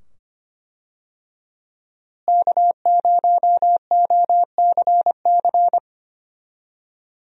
Morse code, heard as a keyed tone, about 25 wpm, 700 Hz. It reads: K0OCC
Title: USA Call Signs - 5 Characters - Repeated 2x - 25wpm